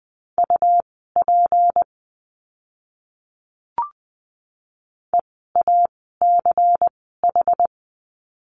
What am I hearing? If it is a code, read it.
Morse code, heard as a keyed tone, about 20 wpm, spaced as for 14 wpm, 700 Hz. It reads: UP EACH
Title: 244 Lesson 39f - Letter z - Words 20wpm